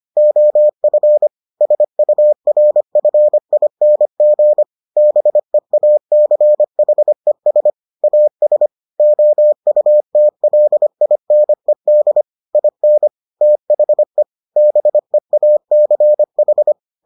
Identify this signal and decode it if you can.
Morse code, heard as a keyed tone, about 25 wpm, 600 Hz. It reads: OF SURFING BEACHES AS OUTLINED IN THE BEACH